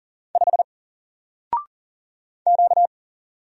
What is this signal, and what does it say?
Morse code, heard as a keyed tone, about 40 wpm, 700 Hz. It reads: 5 Q
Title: Single Letter-Number 40wpm